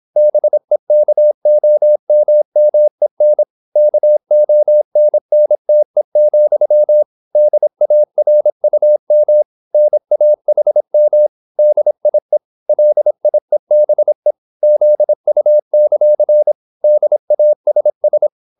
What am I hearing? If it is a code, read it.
Morse code, heard as a keyed tone, about 26 wpm, 600 Hz. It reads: BEKOMMEN KONNTE, DARUM NAHM DIE LIEBE ZU; DASS